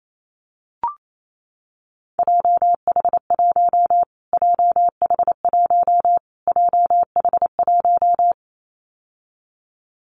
Morse code, heard as a keyed tone, about 28 wpm, 700 Hz. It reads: JH1 JH1 JH1
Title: Callsign Prefixes for Top 25 DXCC - Repeated 3x - 28wpm